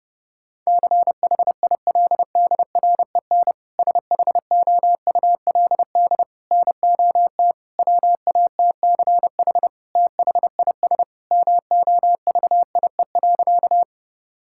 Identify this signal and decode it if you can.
Morse code, heard as a keyed tone, about 28 wpm, 700 Hz. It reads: CHILDREN SHOULD NOT WATCH THIS MOVIE.
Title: Sentences from Top 1000 Words - Review Part 5 - 30wpm